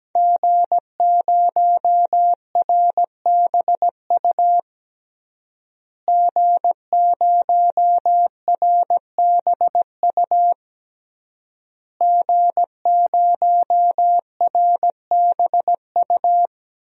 Morse code, heard as a keyed tone, about 17 wpm, 700 Hz. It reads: G0RBU G0RBU G0RBU
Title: DXCC-100 Call Signs - 5 Characters - Repeated 3x - 17wpm